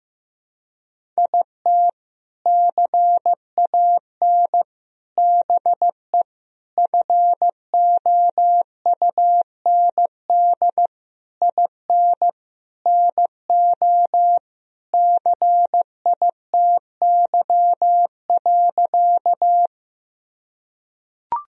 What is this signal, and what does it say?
Morse code, heard as a keyed tone, about 15 wpm, 700 Hz. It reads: IT CAN BE FOUND IN NO CITY.
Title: Sentences from Top 1000 Words - Review Part 1 - 15wpm